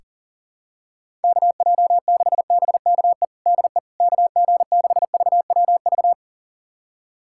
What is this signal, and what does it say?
Morse code, heard as a keyed tone, about 40 wpm, 700 Hz. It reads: KJ6BKE DE KG6VWV
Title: USA Callsign DE Callsign - 6 Characters - 40wpm